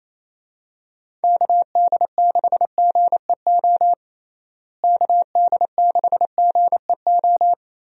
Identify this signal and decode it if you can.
Morse code, heard as a keyed tone, about 27 wpm, 700 Hz. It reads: KD6GEO KD6GEO
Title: USA Call Signs - 6 Characters - Repeated 2x - 28wpm